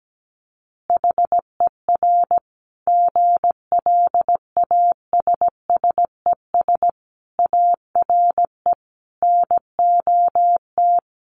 Morse code, heard as a keyed tone, about 17 wpm, 700 Hz. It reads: HER GLASSES ARE NOT
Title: Additional Easy Phrases 17wpm